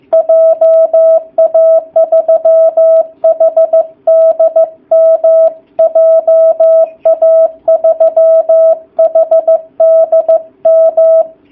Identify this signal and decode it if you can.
Morse code, heard as a keyed tone, about 15 wpm, 660 Hz. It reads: JA3HDM JA3HDM
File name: jh8cbh / 2019cw / sp15- w